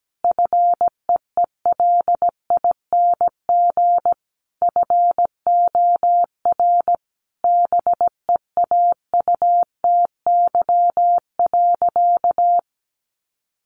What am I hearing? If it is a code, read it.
Morse code, heard as a keyed tone, about 17 wpm, 700 Hz. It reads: FEELING FOR BEAUTY.
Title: Sentences from Top 1200 Words 17wpm